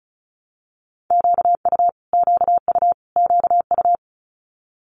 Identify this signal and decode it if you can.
Morse code, heard as a keyed tone, about 36 wpm, 700 Hz. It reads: QU QU QU
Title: Multi-letter Phonemes Repeated 3x 35wpm